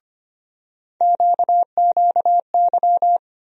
Morse code, heard as a keyed tone, about 25 wpm, 700 Hz. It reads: QQY
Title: USA Call Sign Postfixes - 3 Letters 25wpm